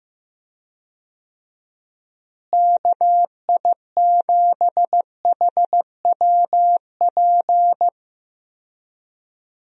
Morse code, heard as a keyed tone, about 15 wpm, 700 Hz. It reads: KI7HWP
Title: USA Call Signs - 6 Characters - Repeated 3x - 15wpm